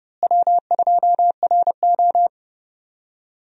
Morse code, heard as a keyed tone, about 32 wpm, 700 Hz. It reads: W2RO
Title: USA Call Signs - 4 Characters - Repeated 3x - 30wpm